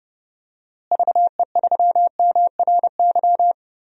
Morse code, heard as a keyed tone, about 30 wpm, 700 Hz. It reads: VE3MRY
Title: DXCC-100 Call Signs - 6 Characters - Repeated 3x - 30wpm